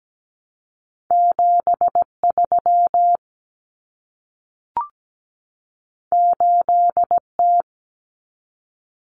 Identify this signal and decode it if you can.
Morse code, heard as a keyed tone, about 17 wpm, 700 Hz. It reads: 73 8T
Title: Common Cut Numbers - 2-Digit 17wpm